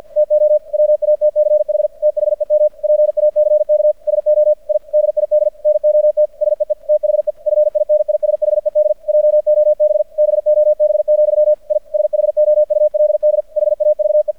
Morse code, heard as a keyed tone, about 50 wpm, 600 Hz. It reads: TO WATCH THEM PACK SO I DID NOT SEE THE FINISHED JOB BOB, I SHOULD HAVE
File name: archived / CRSnet